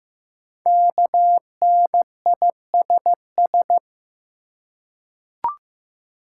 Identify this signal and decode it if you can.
Morse code, heard as a keyed tone, about 15 wpm, 700 Hz. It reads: KNISS
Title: Code Groups - 5 Letters - 15wpm